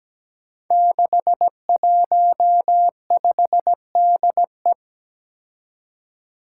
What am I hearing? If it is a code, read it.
Morse code, heard as a keyed tone, about 17 wpm, 700 Hz. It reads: 615DE
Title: Five Challenging Letter-Number Combinations 17wpm